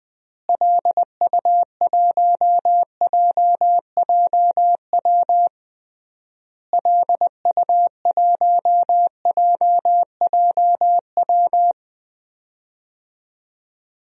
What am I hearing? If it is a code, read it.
Morse code, heard as a keyed tone, about 20 wpm, 700 Hz. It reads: LU1JJW LU1JJW
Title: DXCC-100 Call Signs - 6 Characters - Repeated 3x - 20wpm